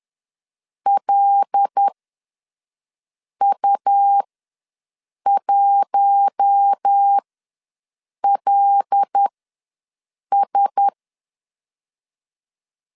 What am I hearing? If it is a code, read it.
Morse code, heard as a keyed tone, about 10 wpm, spaced as for 5 wpm, 800 Hz. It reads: LU1LS
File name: LEZIONI / 5 LBVC / 5 L ws 10wpm 3636